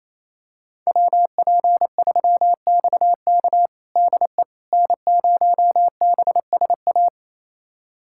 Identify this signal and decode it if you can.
Morse code, heard as a keyed tone, about 27 wpm, 700 Hz. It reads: WP3XK DE N0BSA
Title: USA Callsign DE Callsign - 5 Characters - 28wpm